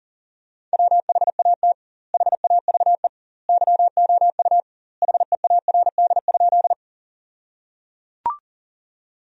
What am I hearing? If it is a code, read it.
Morse code, heard as a keyed tone, about 37 wpm, 700 Hz. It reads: WHAT HAVE YOU HEARD?
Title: 2x Word-Spacing - Sentences from Top 400 Words - 40wpm